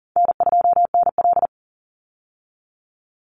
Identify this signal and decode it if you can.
Morse code, heard as a keyed tone, about 40 wpm, 700 Hz. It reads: N2NL
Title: Easy Call Signs Repeated 3x - 40wpm